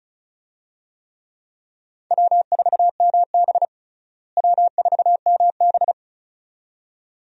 Morse code, heard as a keyed tone, about 36 wpm, 700 Hz. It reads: W4MB W4MB
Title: USA Call Signs - 4 Characters - Repeated 2x - 35wpm